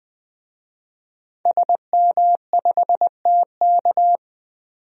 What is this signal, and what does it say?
Morse code, heard as a keyed tone, about 20 wpm, 700 Hz. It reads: SM5TK
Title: DXCC-100 Call Signs - 5 Characters - 20wpm